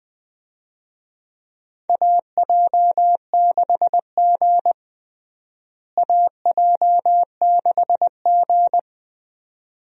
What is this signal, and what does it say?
Morse code, heard as a keyed tone, about 21 wpm, 700 Hz. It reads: AJ6G AJ6G
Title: USA Call Signs - 4 Characters - Repeated 2x - 20wpm